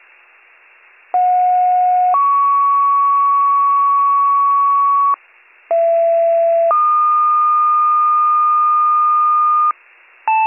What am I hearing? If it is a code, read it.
Two-tone page, heard as a one-second tone, then a three-second tone, about 730 then 1120 Hz, about 670 then 1220 Hz. It is heard from 1.1 to 9.7 s.